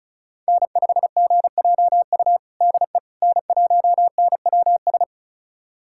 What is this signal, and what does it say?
Morse code, heard as a keyed tone, about 36 wpm, 700 Hz. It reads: N5GJU DE N1NWS
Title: USA Callsign DE Callsign - 5 Characters - 35wpm